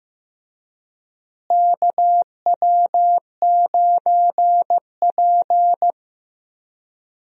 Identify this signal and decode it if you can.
Morse code, heard as a keyed tone, about 15 wpm, 700 Hz. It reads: KW9P
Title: USA Call Signs - 4 Characters - Repeated 3x - 15wpm